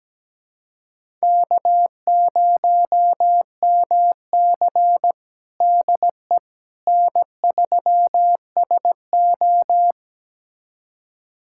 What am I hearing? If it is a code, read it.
Morse code, heard as a keyed tone, about 17 wpm, 700 Hz. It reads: K0MC DE N3SO
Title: USA Callsign DE Callsign - 4 Characters - 17wpm